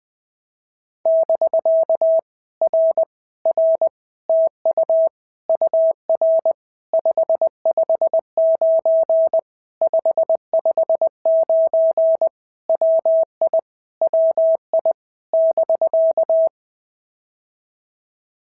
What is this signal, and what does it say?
Morse code, heard as a keyed tone, about 19 wpm, 650 Hz. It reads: <BK> R R TU UR 559 559 WI WI <BK>
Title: POTA QSO Practice - 20wpm